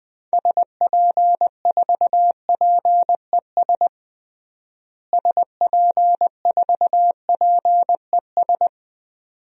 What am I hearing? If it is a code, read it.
Morse code, heard as a keyed tone, about 20 wpm, 700 Hz. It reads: SP4PES SP4PES
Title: DXCC-100 Call Signs - 6 Characters - Repeated 2x - 20wpm